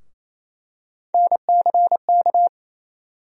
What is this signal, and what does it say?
Morse code, heard as a keyed tone, about 28 wpm, 700 Hz. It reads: NCK
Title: USA Call Sign Postfixes - 3 Letters 28wpm